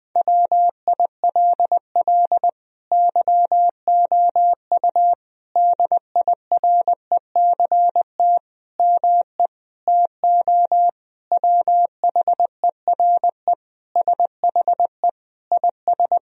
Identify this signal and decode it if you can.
Morse code, heard as a keyed tone, about 20 wpm, 700 Hz. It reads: WILL YOU DIRECT ME TO WHERE SHE IS
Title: Sentences from Top 1000 Words - Review Part 2 - 20wpm